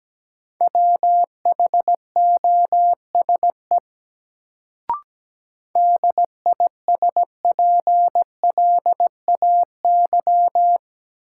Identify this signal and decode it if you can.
Morse code, heard as a keyed tone, about 17 wpm, 700 Hz. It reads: WHOSE DISPLAY
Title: Sentences from Top 1200 Words - New Word Drill Alt - 17wpm